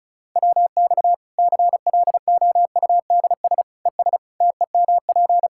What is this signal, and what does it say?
Morse code, heard as a keyed tone, about 34 wpm, 700 Hz. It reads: WX CLOUDS ES TEMP
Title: Practice QSO - Exchange 2 - Info - 35wpm